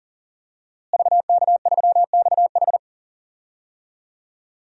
Morse code, heard as a keyed tone, about 40 wpm, 700 Hz. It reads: VK3XH
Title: DXCC-100 Call Signs - 5 Characters - Repeated 3x - 40wpm